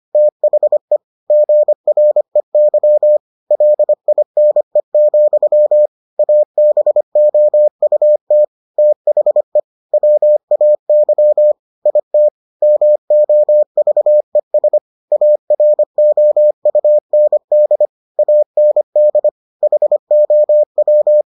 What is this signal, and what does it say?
Morse code, heard as a keyed tone, about 25 wpm, 600 Hz. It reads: THE GREY LINE, ABOUT THE WAY IT MOVES AROUND AND HOW